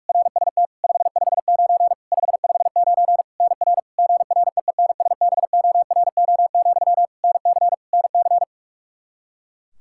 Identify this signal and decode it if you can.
Morse code, heard as a keyed tone, about 42 wpm, 700 Hz. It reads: RST 559 559 NR GREENSBORO, NC NC
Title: K1USN SST QSO Practice 45wpm